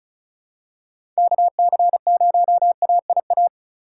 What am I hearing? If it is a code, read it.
Morse code, heard as a keyed tone, about 36 wpm, 700 Hz. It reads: KC0AIA
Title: USA Call Signs - 3x Character Spacing - 35wpm